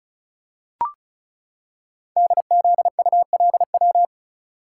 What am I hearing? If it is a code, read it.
Morse code, heard as a keyed tone, about 36 wpm, 690 Hz. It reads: DZULW
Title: Code Groups - 5 Letters - 35wpm